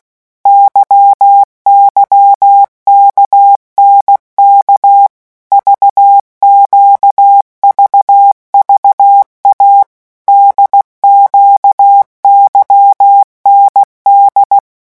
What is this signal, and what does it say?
Morse code, heard as a keyed tone, about 16 wpm, 800 Hz. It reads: YYKNK VQVVA DQYND
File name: morse mp3 / revision3-vitesse 15 mots